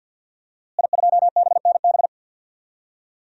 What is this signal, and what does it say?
Morse code, heard as a keyed tone, about 51 wpm, 700 Hz. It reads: I2BNB
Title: DXCC-100 Call Signs - 5 Characters - Repeated 2x - 50wpm